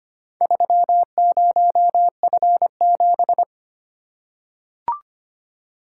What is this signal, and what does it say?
Morse code, heard as a keyed tone, about 25 wpm, 700 Hz. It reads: 30F7
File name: practice / 25wpm / Four Letter-Number Combinations - 25wpm